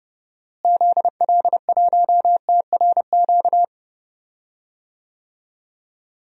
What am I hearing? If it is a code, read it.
Morse code, heard as a keyed tone, about 30 wpm, 700 Hz. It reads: ZL1TRQ
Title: DXCC-100 Call Signs - 6 Characters - Repeated 2x - 30wpm